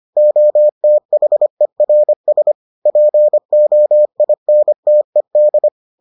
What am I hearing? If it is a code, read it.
Morse code, heard as a keyed tone, about 25 wpm, 600 Hz. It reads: OTHERS POINTED